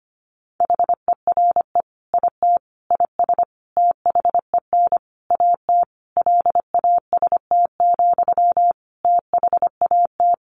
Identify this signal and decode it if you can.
Morse code, heard as a keyed tone, about 25 wpm, 700 Hz. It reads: HERE IT IS THEN AT LAST, THAT
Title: Sentences from Top 1500 Words 25wpm